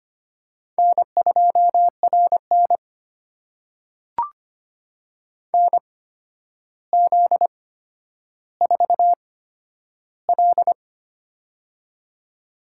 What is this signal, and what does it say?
Morse code, heard as a keyed tone, about 25 wpm, spaced as for 11 wpm, 700 Hz. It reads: N2RN NZ4L
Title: USA Call Signs - 8x Character Spacing - 25wpm